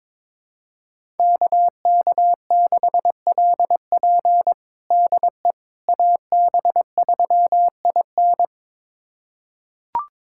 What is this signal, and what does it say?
Morse code, heard as a keyed tone, about 22 wpm, 700 Hz. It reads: KK6LP DE AB3IN
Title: USA Callsign DE Callsign - 5 Characters - 22wpm